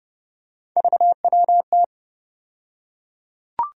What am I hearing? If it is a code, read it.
Morse code, heard as a keyed tone, about 30 wpm, 700 Hz. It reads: VWT
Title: USA Call Sign Postfixes - 3 Letters 30wpm